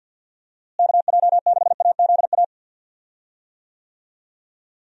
Morse code, heard as a keyed tone, about 52 wpm, 700 Hz. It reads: KJ6AZA
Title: USA Call Signs - 6 Characters - Repeated 3x - 50wpm